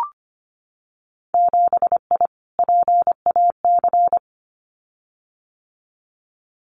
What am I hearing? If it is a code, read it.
Morse code, heard as a keyed tone, about 25 wpm, 700 Hz. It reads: 7I PAC
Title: Winter Field Day Exchanges Encore 25wpm